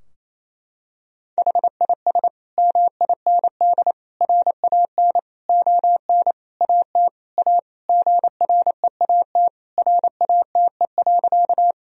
Morse code, heard as a keyed tone, about 28 wpm, 700 Hz. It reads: HIS MIND RAN ON AT A GREAT RATE.
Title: Sentences from Top 1000 Words - Review Part 4 - 28wpm